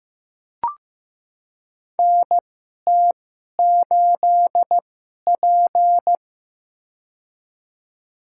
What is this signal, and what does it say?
Morse code, heard as a keyed tone, about 15 wpm, spaced as for 11 wpm, 700 Hz. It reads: NT8P
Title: USA Call Signs - 4 Characters - 2x Character Spacing - 15wpm